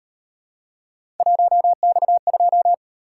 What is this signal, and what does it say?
Morse code, heard as a keyed tone, about 38 wpm, 700 Hz. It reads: 1X2